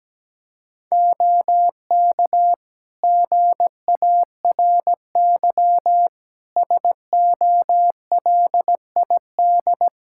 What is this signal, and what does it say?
Morse code, heard as a keyed tone, about 17 wpm, 700 Hz. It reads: OK GARY SOLID
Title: Practice QSO - Exchange 3 - More Info - 17wpm